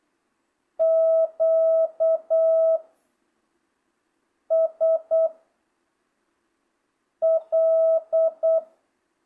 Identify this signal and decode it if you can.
Morse code, heard as a keyed tone, about 8 wpm, 660 Hz. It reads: Q S L